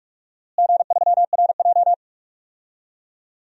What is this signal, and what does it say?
Morse code, heard as a keyed tone, about 46 wpm, 700 Hz. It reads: G3RJ
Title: DXCC-100 Call Signs - 4 Characters - Repeated 3x - 45wpm